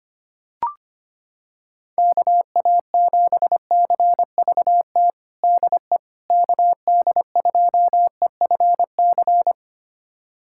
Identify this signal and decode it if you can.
Morse code, heard as a keyed tone, about 25 wpm, 700 Hz. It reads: KA7CVT DE KD2EFC
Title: USA Callsign DE Callsign 25wpm